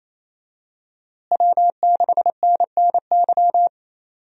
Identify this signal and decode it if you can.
Morse code, heard as a keyed tone, about 28 wpm, 700 Hz. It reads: W6NNY
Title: USA Call Signs - 5 Characters - Repeated 3x - 28wpm